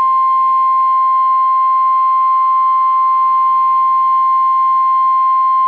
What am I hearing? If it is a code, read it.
Emergency Alert System (SAME) tones (the 1050 Hz attention tone). It is heard from 0.3 to 5.0 s.